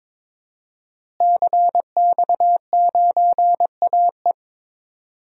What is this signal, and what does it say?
Morse code, heard as a keyed tone, about 22 wpm, 700 Hz. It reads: CX9AE
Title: DXCC-100 Call Signs - 5 Characters - Repeated 2x - 22wpm